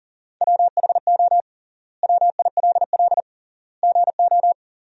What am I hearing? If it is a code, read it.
Morse code, heard as a keyed tone, about 40 wpm, 700 Hz. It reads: WHO WILL GO
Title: 3x Word-Spacing - Sentences from Top 100 Words - 40wpm